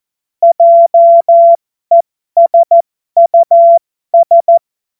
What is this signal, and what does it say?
Morse code, heard as a keyed tone, about 14 wpm, spaced as for 12 wpm, 680 Hz. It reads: JESUS
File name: Morsekurs / Witze